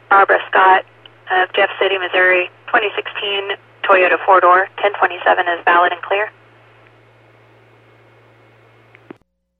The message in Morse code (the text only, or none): none